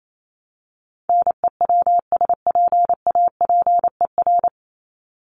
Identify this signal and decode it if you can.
Morse code, heard as a keyed tone, about 28 wpm, 700 Hz. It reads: NEWSPAPER
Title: Top 751-1000 Words 28wpm